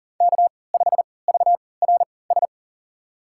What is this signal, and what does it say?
Morse code, heard as a keyed tone, about 39 wpm, spaced as for 22 wpm, 700 Hz. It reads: K5VRS
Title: USA Call Signs - 5 Characters - 3x Character Spacing - 40wpm